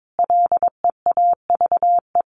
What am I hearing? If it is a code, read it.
Morse code, heard as a keyed tone, about 22 wpm, 700 Hz. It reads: LEAVE
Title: Top 500 Words Repeated 22wpm